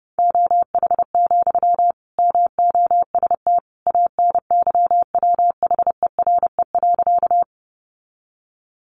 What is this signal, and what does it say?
Morse code, heard as a keyed tone, about 28 wpm, 700 Hz. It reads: OH, MOST ANYWHERE.
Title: Sentences from Top 1400 Words 30wpm